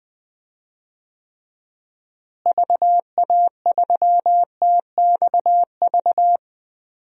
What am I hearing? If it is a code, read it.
Morse code, heard as a keyed tone, about 21 wpm, 700 Hz. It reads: VA3TXV
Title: DXCC-100 Call Signs - 6 Characters - Repeated 2x - 20wpm